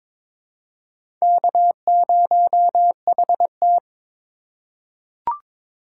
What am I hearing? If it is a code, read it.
Morse code, heard as a keyed tone, about 22 wpm, 700 Hz. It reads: K0HT
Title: K1USN SST QSO Practice - 22wpm